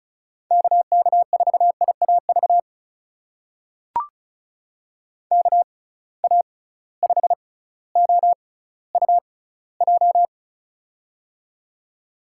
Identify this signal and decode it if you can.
Morse code, heard as a keyed tone, about 36 wpm, spaced as for 18 wpm, 690 Hz. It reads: KK4IAV KA5OUJ
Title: USA Call Signs - 6 Characters - 6x Character Spacing - 35wpm